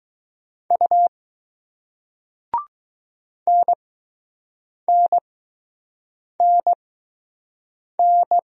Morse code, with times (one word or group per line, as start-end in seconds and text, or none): 0.7-1.1 U
3.5-8.4 NNNN